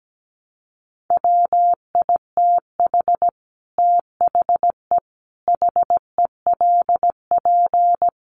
Morse code, heard as a keyed tone, about 17 wpm, 700 Hz. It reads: WITH THE HELP
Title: Sets of 5 Words Encore 17wpm